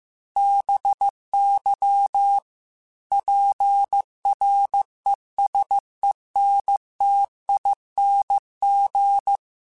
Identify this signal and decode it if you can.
Morse code, heard as a keyed tone, about 15 wpm, 800 Hz. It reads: BY PRESENTING